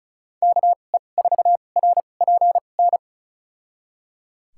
Morse code, heard as a keyed tone, about 36 wpm, spaced as for 25 wpm, 700 Hz. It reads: KE4RPN
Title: USA Call Signs - 6 Characters - 2x Character Spacing - 35wpm